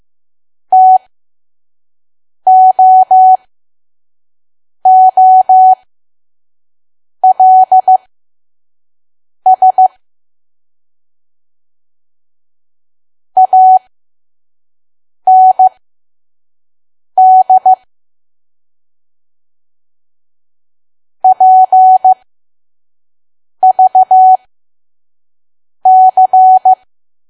Morse code, heard as a keyed tone, about 15 wpm, spaced as for 5 wpm, 750 Hz. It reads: TOOLS AND PVC
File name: Archive / 5 WPM files / PM